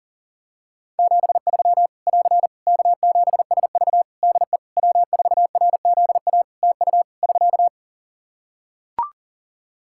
Morse code, heard as a keyed tone, about 37 wpm, 700 Hz. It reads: 73 <AR> K7SV DE W4RZA TU <SK>
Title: Practice QSO - Exchange 4 - Ending - 40wpm